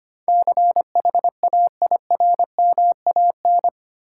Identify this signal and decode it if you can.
Morse code, heard as a keyed tone, about 25 wpm, 700 Hz. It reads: CHAIRMAN
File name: Eight Letter Words - 25wpm